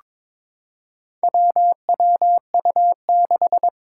Morse code, heard as a keyed tone, about 22 wpm, 700 Hz. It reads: WWU6
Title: Four Challenging Letter-Number Combinations 22wpm